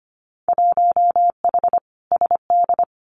Morse code, heard as a keyed tone, about 25 wpm, 690 Hz. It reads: 1H SD
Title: Winter Field Day Exchanges 25wpm